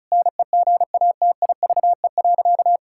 Morse code, heard as a keyed tone, about 34 wpm, 700 Hz. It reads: NEGATIVE.
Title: Sentences from Top 1000 Words - Review Part 6 - 35wpm